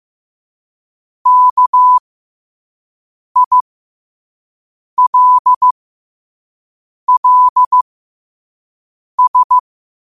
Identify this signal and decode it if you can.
Morse code, heard as a keyed tone, about 15 wpm, spaced as for 5 wpm, 1000 Hz. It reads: KILLS
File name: qotdfiles / 5wpm